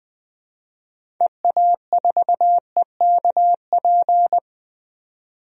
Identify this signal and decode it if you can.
Morse code, heard as a keyed tone, about 21 wpm, 700 Hz. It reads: EA4EKP
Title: DXCC-100 Call Signs - 6 Characters - 20wpm